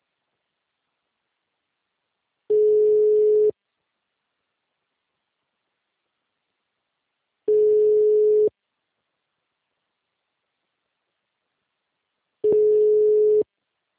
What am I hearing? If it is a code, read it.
Ringback tone (Europe, 425 Hz).